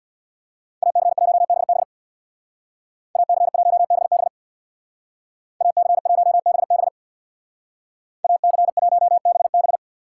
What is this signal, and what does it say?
Morse code, heard as a keyed tone, about 59 wpm, 700 Hz. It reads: AC1BB AC1BB AC1BB AC1BB
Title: Speed Racing - USA 2x2 Call Signs 50wpm